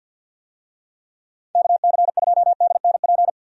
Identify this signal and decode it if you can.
Morse code, heard as a keyed tone, about 53 wpm, 700 Hz. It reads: KC2DNP
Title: USA Call Signs - 6 Characters - 50wpm